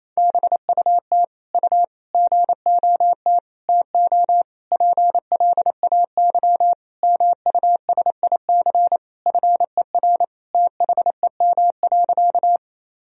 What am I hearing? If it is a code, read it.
Morse code, heard as a keyed tone, about 28 wpm, 700 Hz. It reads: BUT U GOT TO PLAY MUSIC FER THEM.
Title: Sentences from Top 300 Words and Abbreviations 28wpm